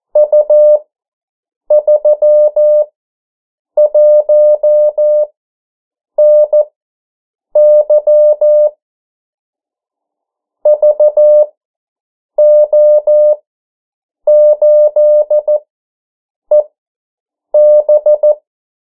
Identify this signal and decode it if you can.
Morse code, heard as a keyed tone, about 14 wpm, spaced as for 7 wpm, 600 Hz. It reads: U31NY VO8EB
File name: morse mp3 / 09 urok 8